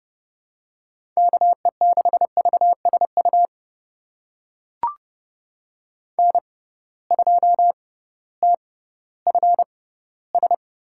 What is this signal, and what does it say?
Morse code, heard as a keyed tone, about 29 wpm, spaced as for 16 wpm, 700 Hz. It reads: KE6VSU N2TFS
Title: USA Call Signs - 6x Character Spacing - 30wpm